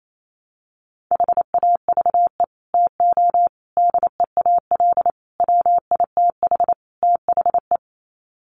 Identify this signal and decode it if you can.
Morse code, heard as a keyed tone, about 28 wpm, 700 Hz. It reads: HAVE TO DEAL WITH THE
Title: Sets of 5 Words Encore2 28wpm